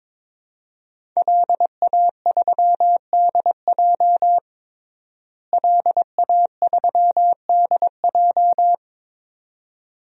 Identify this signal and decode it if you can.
Morse code, heard as a keyed tone, about 22 wpm, 700 Hz. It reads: LA3DJ LA3DJ
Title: DXCC-100 Call Signs - 5 Characters - Repeated 2x - 22wpm